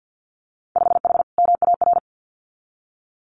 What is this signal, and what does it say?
Morse code, heard as a keyed tone, about 50 wpm, 700 Hz. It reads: 5H MAR
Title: Winter Field Day Exchanges Encore 50wpm